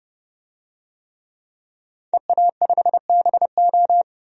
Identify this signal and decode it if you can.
Morse code, heard as a keyed tone, about 31 wpm, 700 Hz. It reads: EA5BO
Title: DXCC-100 Call Signs - 5 Characters - 30wpm